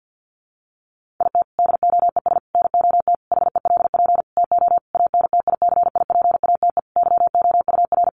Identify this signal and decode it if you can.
Morse code, heard as a keyed tone, about 50 wpm, 700 Hz. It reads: IT DOES NOT HELP TO ANTICIPATE YOUR
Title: Sentences from Top 1500 Words 50wpm